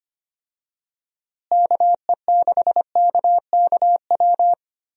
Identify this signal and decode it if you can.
Morse code, heard as a keyed tone, about 25 wpm, 700 Hz. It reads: KE6KKW
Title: USA Call Signs - 6 Characters - Repeated 3x - 25wpm